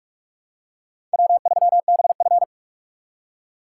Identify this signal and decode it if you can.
Morse code, heard as a keyed tone, about 44 wpm, 700 Hz. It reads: W3BF